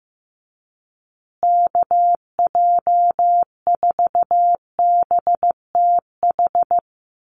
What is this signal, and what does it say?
Morse code, heard as a keyed tone, about 15 wpm, 700 Hz. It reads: KJ4BTH
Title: USA Call Signs Repeated 3x 15wpm